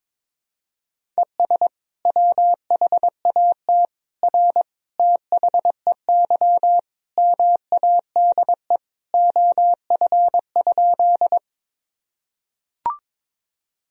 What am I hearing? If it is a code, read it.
Morse code, heard as a keyed tone, about 22 wpm, 700 Hz. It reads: ES WHAT R THEY MADE OF?
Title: Sentences from Top 200 Words and Abbreviations 22wpm